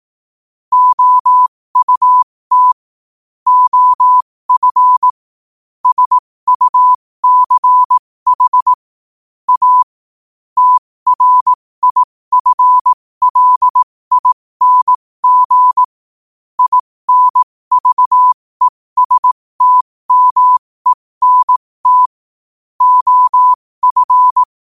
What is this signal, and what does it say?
Morse code, heard as a keyed tone, about 18 wpm, spaced as for 15 wpm, 1000 Hz. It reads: OUT OF SUCH A TRIFLING INVESTMENT OF
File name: qotdfiles / 15wpm